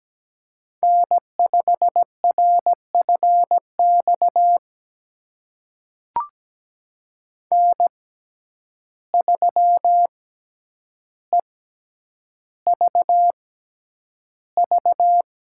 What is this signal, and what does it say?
Morse code, heard as a keyed tone, about 17 wpm, spaced as for 9 wpm, 700 Hz. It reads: N5RFX N3EVV
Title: USA Call Signs - 5 Characters - 6x Character Spacing - 17wpm